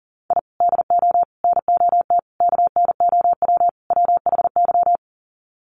Morse code, heard as a keyed tone, about 40 wpm, 700 Hz. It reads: I DO NOT KNOW WHY
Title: Sets of 5 Words 40wpm